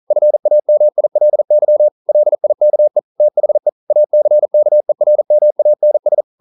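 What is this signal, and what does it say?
Morse code, heard as a keyed tone, about 42 wpm, 600 Hz. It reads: FAMILY LIKE THE ACKERMANS
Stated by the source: Morse code practice at 41 wpm, from CW med Gnister.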